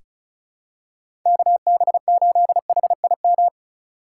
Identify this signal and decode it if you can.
Morse code, heard as a keyed tone, about 36 wpm, 700 Hz. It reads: KB8HIM
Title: USA Call Signs - 6 Characters - 3x Character Spacing - 35wpm